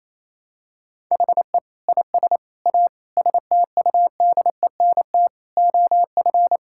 Morse code, heard as a keyed tone, about 28 wpm, 700 Hz. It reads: HE IS A STUDENT OF
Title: Sentences from Top 1000 Words - Review Part 6 - 28wpm